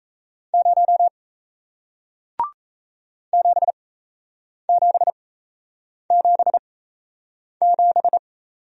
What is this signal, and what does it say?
Morse code, heard as a keyed tone, about 37 wpm, spaced as for 8 wpm, 700 Hz. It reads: 0 7777